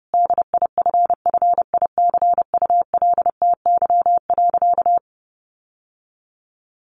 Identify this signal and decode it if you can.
Morse code, heard as a keyed tone, about 30 wpm, 700 Hz. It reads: DIFFICULTY.
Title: Sentences from Top 1500 Words 30wpm